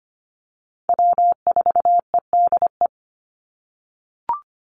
Morse code, heard as a keyed tone, about 25 wpm, 700 Hz. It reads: W4EDE
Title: Call Signs - 25wpm